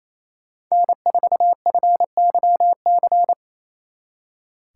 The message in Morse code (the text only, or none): N4FYC